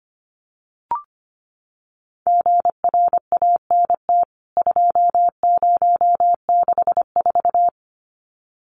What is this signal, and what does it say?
Morse code, heard as a keyed tone, about 25 wpm, 700 Hz. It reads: GRANT 2064
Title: CWT Exchange 25wpm